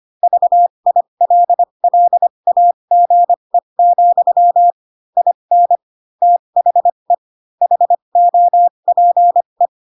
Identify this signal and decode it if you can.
Morse code, heard as a keyed tone, about 25 wpm, spaced as for 22 wpm, 700 Hz. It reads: VILLAGE, IN THE HOPE